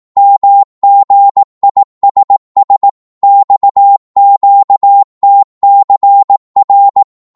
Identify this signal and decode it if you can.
Morse code, heard as a keyed tone, about 18 wpm, 800 Hz. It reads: MGISS XQTCR
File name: Morse / ltr-18wpm